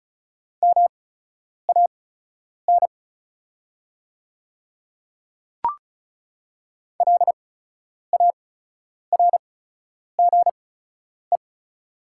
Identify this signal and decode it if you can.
Morse code, heard as a keyed tone, about 34 wpm, spaced as for 10 wpm, 700 Hz. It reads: MAN LARGE
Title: 8x Character Spacing - Top 300 Words - 35wpm